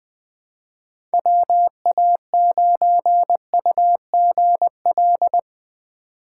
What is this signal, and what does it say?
Morse code, heard as a keyed tone, about 20 wpm, 700 Hz. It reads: WA9UGL
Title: USA Call Signs - 6 Characters - 2x Character Spacing - 20wpm